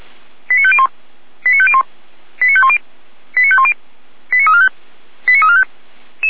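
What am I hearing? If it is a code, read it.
Five-tone selective call (ZVEI), heard as quick runs of five tones, tones of about 70 ms. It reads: 89501 89501 89510 89510 89345 89345
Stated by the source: ZVEI five-tone selective call received by SDR, from feuerwehr-orlamuende (Orlamünde fire brigade) audio archive.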